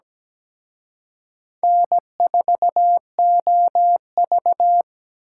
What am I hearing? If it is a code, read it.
Morse code, heard as a keyed tone, about 20 wpm, 700 Hz. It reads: N4OV
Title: Speed Racing - USA Call Signs 17wpm